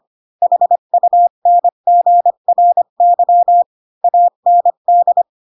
Morse code, heard as a keyed tone, about 25 wpm, 700 Hz. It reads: HUNGRY AND
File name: Pinocchio / Pinocchio ch5 20 25 ik1hge